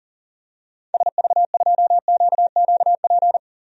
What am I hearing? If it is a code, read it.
Morse code, heard as a keyed tone, about 40 wpm, 700 Hz. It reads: SV2QQP
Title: DXCC-100 Call Signs - 6 Characters - 40wpm